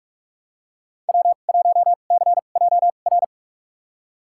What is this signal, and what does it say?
Morse code, heard as a keyed tone, about 46 wpm, spaced as for 32 wpm, 700 Hz. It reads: W1CJR
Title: USA Call Signs - 5 Characters - 2x Character Spacing - 45wpm